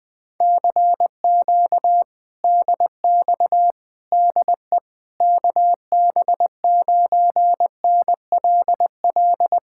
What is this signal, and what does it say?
Morse code, heard as a keyed tone, about 20 wpm, 700 Hz. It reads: CQ DX DE KB9NLL
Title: Speed-Racing Common QSO Phrases - 17wpm